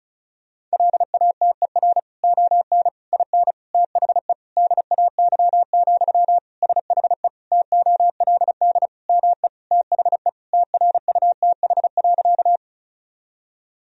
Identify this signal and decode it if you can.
Morse code, heard as a keyed tone, about 36 wpm, 700 Hz. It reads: LATER ON IN THE DAY, SHE TOLD ME THE TRUTH.
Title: Sentences from Top 1000 Words - Review Part 4 - 35wpm